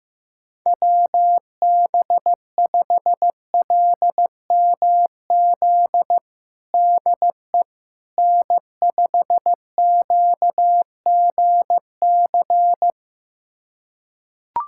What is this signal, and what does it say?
Morse code, heard as a keyed tone, about 15 wpm, 700 Hz. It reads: WB5LMZ DE N5QGC
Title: USA Callsign DE Callsign 15wpm